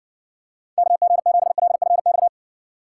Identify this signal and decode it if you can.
Morse code, heard as a keyed tone, about 60 wpm, 700 Hz. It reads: KG7LFX